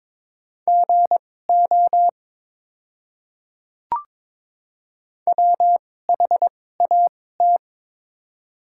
Morse code, heard as a keyed tone, about 22 wpm, spaced as for 16 wpm, 700 Hz. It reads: GO WHAT
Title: 2x Character Spacing - Top 100 Words - 22wpm